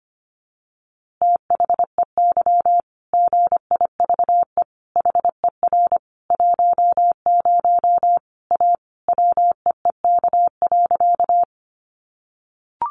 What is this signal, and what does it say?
Morse code, heard as a keyed tone, about 25 wpm, 700 Hz. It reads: THEY GIVE HER 10 A WEEK.
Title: Easy Phrases 25wpm